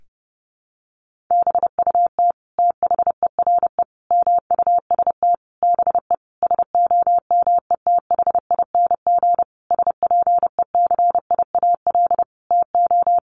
Morse code, heard as a keyed tone, about 30 wpm, 700 Hz. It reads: BUT THERE MUST BE SOMETHING SPECIAL TO
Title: Sentences from Top 1500 Words 30wpm